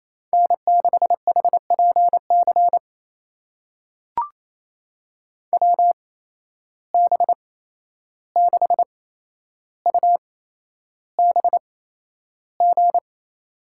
Morse code, heard as a keyed tone, about 28 wpm, spaced as for 11 wpm, 700 Hz. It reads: N6HPC WB6UBG
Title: USA Call Signs - 8x Character Spacing - 28wpm